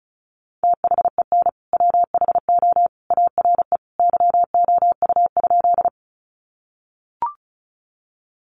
Morse code, heard as a keyed tone, about 34 wpm, 700 Hz. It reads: THEN WHO ARE YOU?
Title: Sentences from Top 100 Words 35wpm